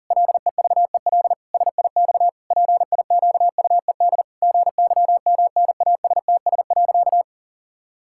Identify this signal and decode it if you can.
Morse code, heard as a keyed tone, about 40 wpm, 700 Hz. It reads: LEVEL SIX PIQUED GYMNASTS.
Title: Pangram Challenge v2 - 40wpm